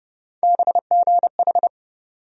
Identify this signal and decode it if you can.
Morse code, heard as a keyed tone, about 30 wpm, 700 Hz. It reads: BGH
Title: USA Call Sign Postfixes - 3 Letters 30wpm